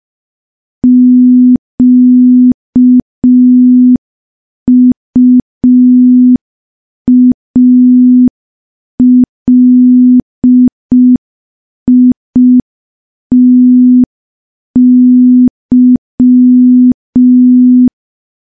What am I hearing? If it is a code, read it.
Morse code, heard as a keyed tone, about 5 wpm, 250 Hz. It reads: QUALITY
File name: morse / morse-training-05WPM